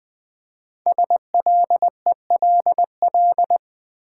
Morse code, heard as a keyed tone, about 20 wpm, 700 Hz. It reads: SLELL
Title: Code Groups - 5 Letters - 20wpm